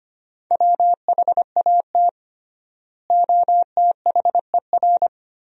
Morse code, heard as a keyed tone, about 25 wpm, 700 Hz. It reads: WHAT OTHER
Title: 3x Word-Spacing - Sentences from Top 100 Words - 25wpm